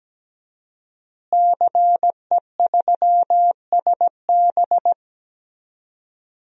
Morse code, heard as a keyed tone, about 17 wpm, 700 Hz. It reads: CE3SB
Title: DXCC-100 Call Signs - 5 Characters - Repeated 2x - 17wpm